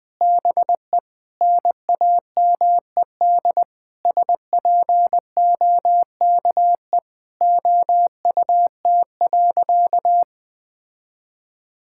Morse code, heard as a keyed tone, about 20 wpm, 700 Hz. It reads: BE NAMED SPOKE OUT.
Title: Sentences with 7-Word Phrases - Spoken Beginning - 20wpm